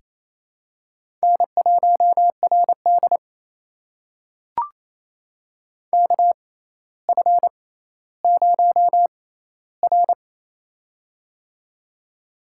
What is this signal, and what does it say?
Morse code, heard as a keyed tone, about 27 wpm, spaced as for 14 wpm, 700 Hz. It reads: N1RD KF0R
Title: USA Call Signs - 4 Characters - 6x Character Spacing - 28wpm